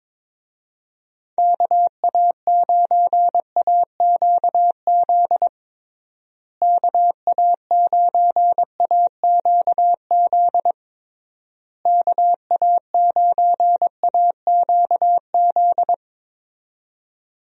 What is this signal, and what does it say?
Morse code, heard as a keyed tone, about 22 wpm, 700 Hz. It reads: KA9AQZ KA9AQZ KA9AQZ
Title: USA Call Signs - 6 Characters - Repeated 3x - 22wpm